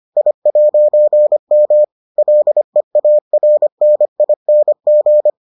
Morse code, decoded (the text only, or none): I'M LEARNING